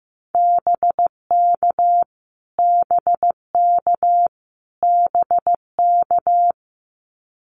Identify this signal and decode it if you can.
Morse code, heard as a keyed tone, about 15 wpm, 700 Hz. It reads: BK BK BK
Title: Abbreviations Repeated 3x - 15wpm